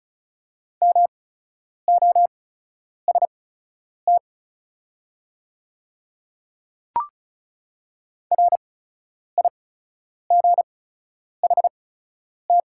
Morse code, heard as a keyed tone, about 36 wpm, spaced as for 10 wpm, 700 Hz. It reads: MOST RIGHT
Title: 8x Character Spacing - Top 200 Words - 35wpm